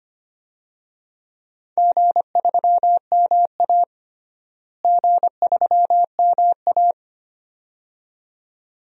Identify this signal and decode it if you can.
Morse code, heard as a keyed tone, about 25 wpm, 700 Hz. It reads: G3MA G3MA
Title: DXCC-100 Call Signs - 4 Characters - Repeated 2x - 25wpm